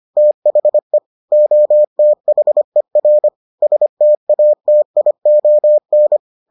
Morse code, heard as a keyed tone, about 25 wpm, 600 Hz. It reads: THE OTHER STATION